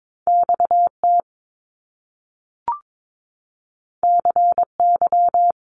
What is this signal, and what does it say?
Morse code, heard as a keyed tone, about 22 wpm, 700 Hz. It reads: XT CY
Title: Top Two Letter Combinations in Words 22wpm